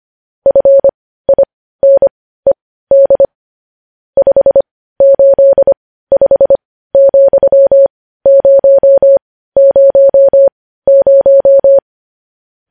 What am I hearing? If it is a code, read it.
Morse code, heard as a keyed tone, about 25 wpm, spaced as for 15 wpm, 550 Hz. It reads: FINED 585,000
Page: news-headlines-in-morse-code-at-15-wpm